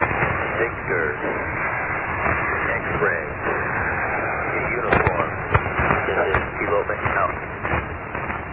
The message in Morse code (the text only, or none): none